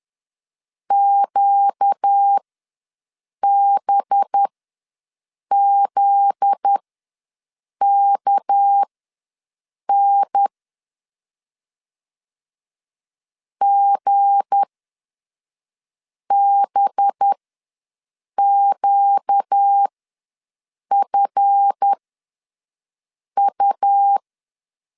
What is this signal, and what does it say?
Morse code, heard as a keyed tone, about 10 wpm, spaced as for 5 wpm, 800 Hz. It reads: QBZKN GBQFU
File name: LEZIONI / 10 SIGNS / 10B INT ws 10wpm 4134